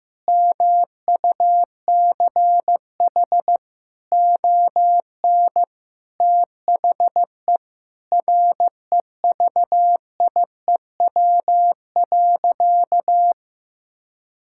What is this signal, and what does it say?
Morse code, heard as a keyed tone, about 15 wpm, 700 Hz. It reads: MUCH ON THE REVIEW.
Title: Sentences from Top 1000 Words - Review Part 6 - 15wpm